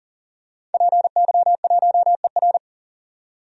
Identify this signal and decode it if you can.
Morse code, heard as a keyed tone, about 40 wpm, 700 Hz. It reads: PY1ER
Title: DXCC-100 Call Signs - 5 Characters - 40wpm